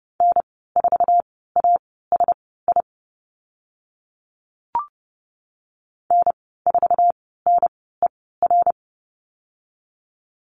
Morse code, heard as a keyed tone, about 30 wpm, spaced as for 17 wpm, 700 Hz. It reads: N4ASI N4NER
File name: Lesson 14g - Number 2 - Call Signs 30wpm